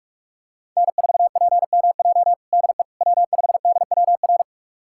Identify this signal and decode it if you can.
Morse code, heard as a keyed tone, about 46 wpm, 700 Hz. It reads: N4PMJ DE W5DWR
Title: USA Callsign DE Callsign - 5 Characters - 45wpm